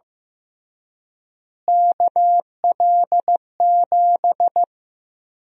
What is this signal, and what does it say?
Morse code, heard as a keyed tone, about 15 wpm, 700 Hz. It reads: KL7
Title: USA Call Sign Prefixes - Repeated 2x - 15wpm